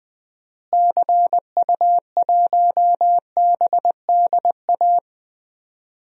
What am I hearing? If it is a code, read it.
Morse code, heard as a keyed tone, about 20 wpm, 700 Hz. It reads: CU1BDA
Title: DXCC-100 Call Signs - 6 Characters - Repeated 2x - 20wpm